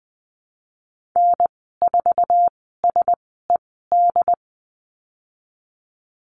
Morse code, heard as a keyed tone, about 20 wpm, spaced as for 15 wpm, 700 Hz. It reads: N4SED
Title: 076 Lesson 13g - Letter L - Call Signs 20wpm